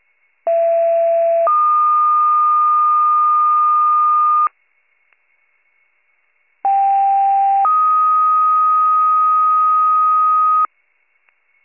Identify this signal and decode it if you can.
Two-tone page, heard as a one-second tone, then a three-second tone, about 670 then 1220 Hz, about 790 then 1250 Hz. It is heard from 0.5 to 10.7 s.